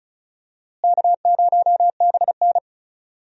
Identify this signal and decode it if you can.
Morse code, heard as a keyed tone, about 34 wpm, 700 Hz. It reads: K0BN
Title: USA Call Signs - 4 Characters - Repeated 3x - 35wpm